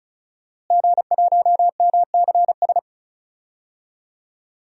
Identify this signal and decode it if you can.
Morse code, heard as a keyed tone, about 36 wpm, 700 Hz. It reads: G1MCS
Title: DXCC-100 Call Signs - 5 Characters - Repeated 3x - 35wpm